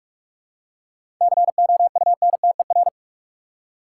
Morse code, heard as a keyed tone, about 44 wpm, 700 Hz. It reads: COUNTER
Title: Seven Letter Words - 45wpm